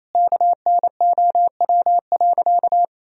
Morse code, heard as a keyed tone, about 28 wpm, 700 Hz. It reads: KNOW.
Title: Sentences from Top 1000 Words - Review Part 3 - 28wpm